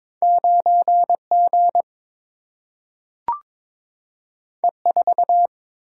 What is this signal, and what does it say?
Morse code, heard as a keyed tone, about 22 wpm, 700 Hz. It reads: 9G E4
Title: Single-Letter Single-Number 22wpm